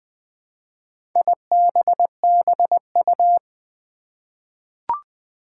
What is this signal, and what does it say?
Morse code, heard as a keyed tone, about 20 wpm, 700 Hz. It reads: IBBU
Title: Four Challenging Letter-Number Combinations 20wpm